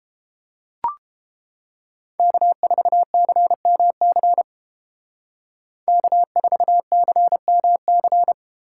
Morse code, heard as a keyed tone, about 29 wpm, 700 Hz. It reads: K4CMC K4CMC